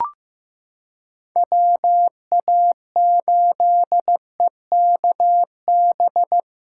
Morse code, heard as a keyed tone, about 15 wpm, 700 Hz. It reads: WA8EKB
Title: USA Callsign DE Callsign - 6 Characters - 15wpm